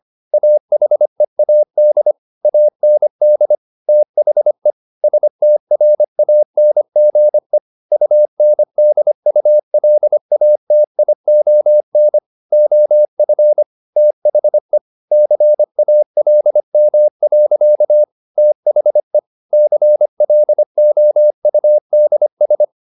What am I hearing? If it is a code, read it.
Morse code, heard as a keyed tone, about 25 wpm, 600 Hz. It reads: AHEAD AND THE STRANGE UNDULATION OF THE CALM. THE CLOUDS
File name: the old man and the sea part 1 25wpm